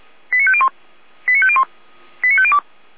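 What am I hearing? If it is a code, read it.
Five-tone selective call (ZVEI), heard as quick runs of five tones, tones of about 70 ms. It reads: 89501 89501 89502
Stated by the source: ZVEI five-tone selective call received by SDR, archived by Feuerwehr Orlamünde (fire brigade).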